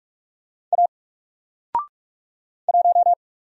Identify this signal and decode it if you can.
Morse code, heard as a keyed tone, about 47 wpm, 700 Hz. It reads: A 1
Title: Single Character (Letter:Number:Punctuation:Prosign) 45wpm